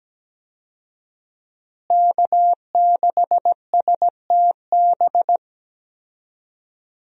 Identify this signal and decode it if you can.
Morse code, heard as a keyed tone, about 17 wpm, 700 Hz. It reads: K6STB
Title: USA Call Signs - 5 Characters - Repeated 3x - 17wpm